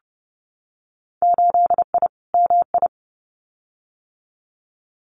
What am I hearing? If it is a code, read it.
Morse code, heard as a keyed tone, about 30 wpm, 700 Hz. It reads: 8I MI
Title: Winter Field Day Exchanges Encore 30wpm